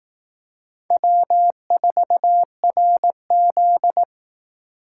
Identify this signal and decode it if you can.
Morse code, heard as a keyed tone, about 18 wpm, 700 Hz. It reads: W4RZ
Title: Speed Racing - USA Call Signs 15wpm